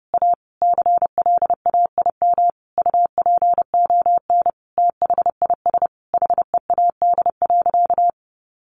Morse code, heard as a keyed tone, about 30 wpm, 700 Hz. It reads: A CLAIM UPON THIS HEAD.
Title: Sentences from Top 1500 Words 30wpm